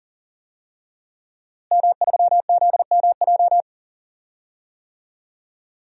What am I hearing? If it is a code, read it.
Morse code, heard as a keyed tone, about 40 wpm, 700 Hz. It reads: M3ZMJ
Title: DXCC-100 Call Signs - 5 Characters - Repeated 3x - 40wpm